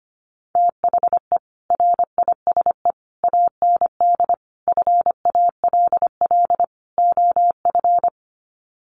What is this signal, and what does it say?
Morse code, heard as a keyed tone, about 25 wpm, 700 Hz. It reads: THE RISE AND FALL OF
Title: Sets of 5 Words Encore2 25wpm